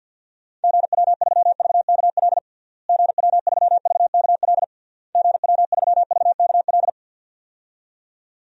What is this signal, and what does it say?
Morse code, heard as a keyed tone, about 47 wpm, 700 Hz. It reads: GW3VKL GW3VKL GW3VKL
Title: DXCC-100 Call Signs - 6 Characters - Repeated 3x - 50wpm